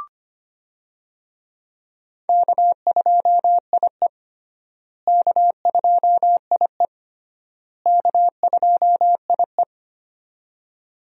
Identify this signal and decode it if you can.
Morse code, heard as a keyed tone, about 25 wpm, 700 Hz. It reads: K2IE K2IE K2IE
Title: USA Call Signs - 4 Characters - Repeated 3x - 25wpm